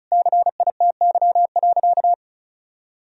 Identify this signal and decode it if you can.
Morse code, heard as a keyed tone, about 36 wpm, 700 Hz. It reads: CITY.
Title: Sentences from Top 1000 Words - Review Part 2 - 35wpm